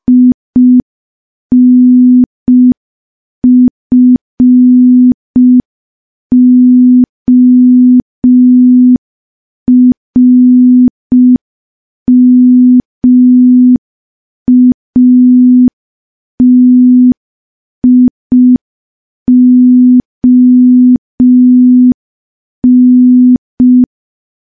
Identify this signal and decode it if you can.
Morse code, heard as a keyed tone, about 5 wpm, 250 Hz. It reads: INFORMATION
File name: morse / morse-training-05WPM